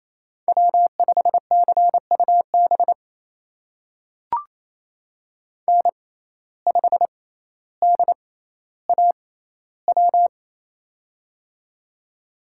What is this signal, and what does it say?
Morse code, heard as a keyed tone, about 28 wpm, spaced as for 14 wpm, 700 Hz. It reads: W5CUB N5DAW
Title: USA Call Signs - 5 Characters - 6x Character Spacing - 28wpm